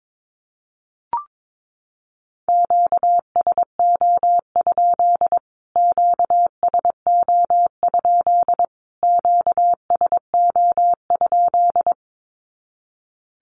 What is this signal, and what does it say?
Morse code, heard as a keyed tone, about 22 wpm, 700 Hz. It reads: QSO? QSO? QSO?
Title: Q-Codes Repeated 3x 22wpm